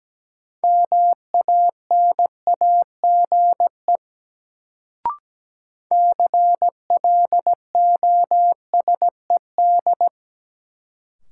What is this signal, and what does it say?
Morse code, heard as a keyed tone, about 17 wpm, 700 Hz. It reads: MANAGE CLOSED
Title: Sentences from Top 1400 Words - New Word Drill - 17wpm